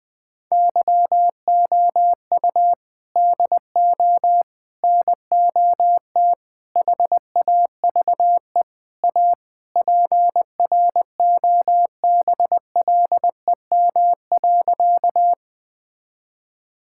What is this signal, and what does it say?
Morse code, heard as a keyed tone, about 20 wpm, 700 Hz. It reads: YOU DO NOT HAVE A PROBLEM.
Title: Sentences with 5-Word Phrases - 20wpm